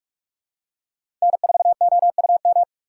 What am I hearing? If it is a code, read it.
Morse code, heard as a keyed tone, about 43 wpm, 690 Hz. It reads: N4OUM
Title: USA Call Signs - 5 Characters - 2x Character Spacing - 45wpm